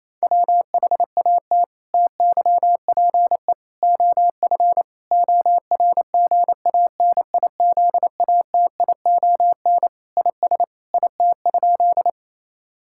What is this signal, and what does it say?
Morse code, heard as a keyed tone, about 28 wpm, 700 Hz. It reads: WHAT TYPE OF ORGANIZATION IS IT?
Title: Sentences from Top 1000 Words - Review Part 5 - 28wpm